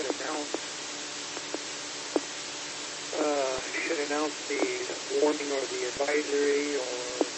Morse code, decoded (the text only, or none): none